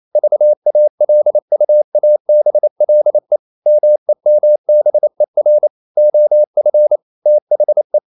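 Morse code, heard as a keyed tone, about 28 wpm, 600 Hz. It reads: VALUABLE MEMBER OF THE
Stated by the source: Morse code practice at 28 wpm, text of Ken Follett's novel Never in English, from CW med Gnister.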